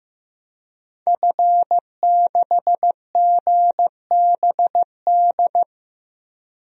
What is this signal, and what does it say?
Morse code, heard as a keyed tone, about 16 wpm, 700 Hz. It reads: F6GBD
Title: DXCC-100 Call Signs - 5 Characters - 15wpm